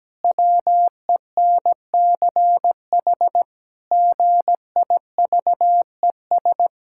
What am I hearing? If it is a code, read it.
Morse code, heard as a keyed tone, about 17 wpm, 700 Hz. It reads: WENCH GIVES
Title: Pangram Challenge v2 - 17wpm